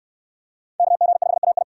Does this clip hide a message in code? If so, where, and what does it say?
Yes: Morse code from 0.8 to 1.6 s, heard as a keyed tone, about 67 wpm, 700 Hz. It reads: KG5AI